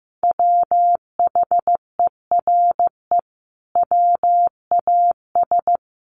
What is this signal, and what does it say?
Morse code, heard as a keyed tone, about 14 wpm, 700 Hz. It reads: WHERE WAS
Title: Sentences from Top 1000 Words 15wpm